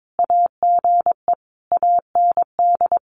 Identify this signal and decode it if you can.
Morse code, heard as a keyed tone, about 22 wpm, 700 Hz. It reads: AGE AND
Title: Sentences from Top 1300 Words 22wpm